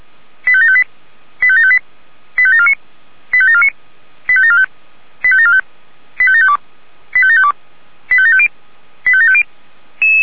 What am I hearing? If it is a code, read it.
Five-tone selective call (ZVEI), heard as quick runs of five tones, tones of about 80 ms. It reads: 85758 85758 85749 85749 85745 85745 85752 85752 85750 85750
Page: SDR/ZVEI